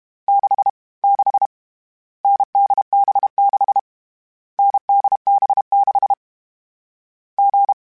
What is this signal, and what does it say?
Morse code, heard as a keyed tone, about 32 wpm, 800 Hz. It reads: 6 6 NDB6 NDB6 G
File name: Difficult Combos / Difficult Combos-mp3s / awj1 ndb6 gz7 @30